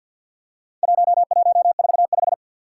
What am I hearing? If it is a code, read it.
Morse code, heard as a keyed tone, about 48 wpm, 700 Hz. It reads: 1145
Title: Four Challenging Letter-Number Combinations 50wpm